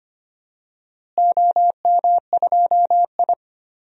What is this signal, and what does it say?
Morse code, heard as a keyed tone, about 25 wpm, 700 Hz. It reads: OM2I
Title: DXCC-100 Call Signs - 4 Characters - Repeated 3x - 25wpm